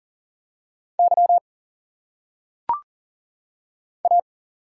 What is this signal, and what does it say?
Morse code, heard as a keyed tone, about 40 wpm, 700 Hz. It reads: Y A
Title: Single Letters - Phonetic Pronunciation 40wpm